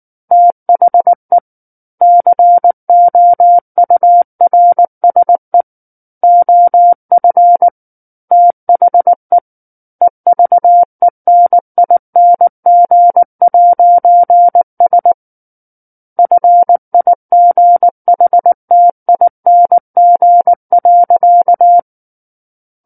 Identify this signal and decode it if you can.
Morse code, heard as a keyed tone, about 19 wpm, 700 Hz. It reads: THE COURSE OF THE EVENING'S FIGHTING.
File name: Book One / 10-Chapter 10 - 19 WPM